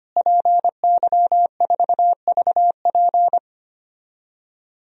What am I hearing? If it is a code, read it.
Morse code, heard as a keyed tone, about 25 wpm, 700 Hz. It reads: PY4VP
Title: DXCC-100 Call Signs - 5 Characters - Repeated 3x - 25wpm